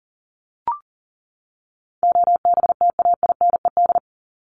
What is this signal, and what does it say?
Morse code, heard as a keyed tone, about 40 wpm, 700 Hz. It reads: OBTAINED
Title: ED Suffix - Top 100 Words with Base Words 40wpm